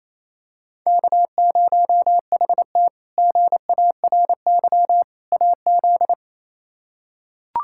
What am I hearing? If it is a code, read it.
Morse code, heard as a keyed tone, about 27 wpm, 700 Hz. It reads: K0HT GARY AZ
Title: K1USN SST QSO Practice - 28wpm